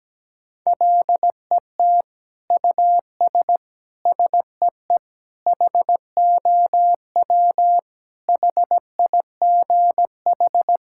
Morse code, heard as a keyed tone, about 17 wpm, 700 Hz. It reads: LET US SEE HOW HIGH
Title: Sentences from Top 1000 Words - Review Part 6 - 17wpm